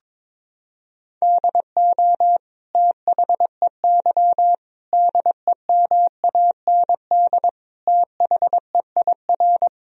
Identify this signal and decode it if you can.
Morse code, heard as a keyed tone, about 22 wpm, 700 Hz. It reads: DO THEY DEMAND THEIR
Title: Sentences from Top 1000 Words - Review Part 3 - 22wpm